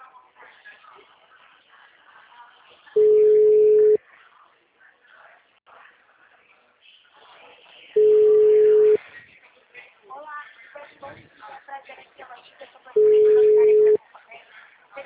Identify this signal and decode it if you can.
Ringback tone (Europe, 425 Hz), with speech.
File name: out- -2017- - - .49045